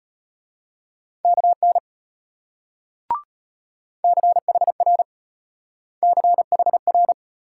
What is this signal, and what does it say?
Morse code, heard as a keyed tone, about 31 wpm, 700 Hz. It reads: KN CHR CHR
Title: Speed-Racing Multi-letter Phonemes - 25wpm